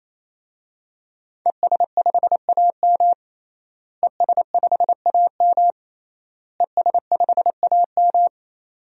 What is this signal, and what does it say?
Morse code, heard as a keyed tone, about 28 wpm, 700 Hz. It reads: ES5AM ES5AM ES5AM
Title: DXCC-100 Call Signs - 5 Characters - Repeated 3x - 28wpm